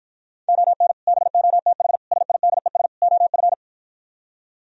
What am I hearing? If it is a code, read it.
Morse code, heard as a keyed tone, about 52 wpm, 700 Hz. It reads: ON BOTH SIDES OF